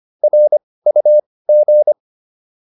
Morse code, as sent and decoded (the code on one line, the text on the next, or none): .-. ..- --.
RUG